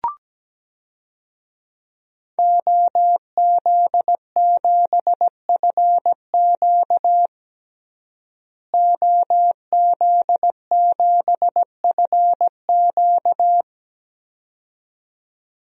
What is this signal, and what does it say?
Morse code, heard as a keyed tone, about 17 wpm, 700 Hz. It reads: OZ7FQ OZ7FQ
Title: DXCC-100 Call Signs - 5 Characters - Repeated 2x - 17wpm